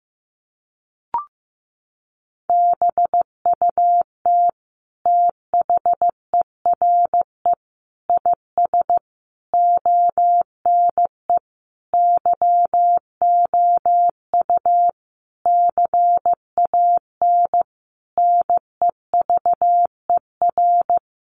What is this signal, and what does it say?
Morse code, heard as a keyed tone, about 15 wpm, 700 Hz. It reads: BUT THERE IS ONE YOU CAN NEVER
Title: Sentences from Top 1200 Words 15wpm